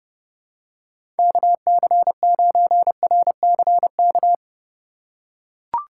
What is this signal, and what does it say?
Morse code, heard as a keyed tone, about 30 wpm, 700 Hz. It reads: KC9RCK
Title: USA Call Signs - 6 Characters - 3x Character Spacing - 30wpm